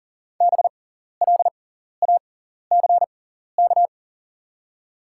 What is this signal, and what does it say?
Morse code, heard as a keyed tone, about 38 wpm, spaced as for 14 wpm, 700 Hz. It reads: BLACK
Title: 6x Character Spacing - Top 500 Words - 40wpm